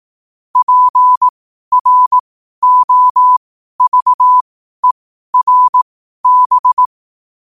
Morse code, heard as a keyed tone, about 18 wpm, spaced as for 12 wpm, 1000 Hz. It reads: PROVERB
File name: qotdfiles / 12wpm